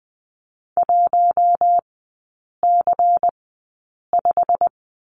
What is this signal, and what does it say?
Morse code, heard as a keyed tone, about 20 wpm, spaced as for 8 wpm, 700 Hz. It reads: 1C5
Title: 093 Lesson 16e - Letter c - Three Characters 20wpm